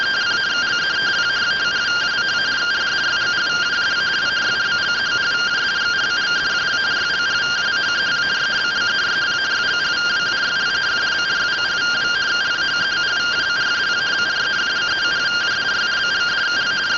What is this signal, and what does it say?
Two-tone teleprinter signal, about 170 Hz shift.